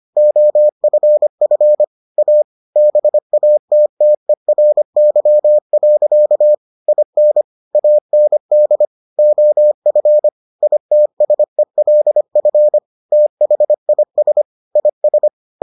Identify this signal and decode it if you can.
Morse code, heard as a keyed tone, about 25 wpm, 600 Hz. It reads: OFF A BATTERY. IN AND OF ITSELF THIS IS